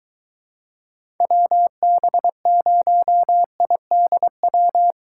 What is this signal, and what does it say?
Morse code, heard as a keyed tone, about 21 wpm, 700 Hz. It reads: WB0IDW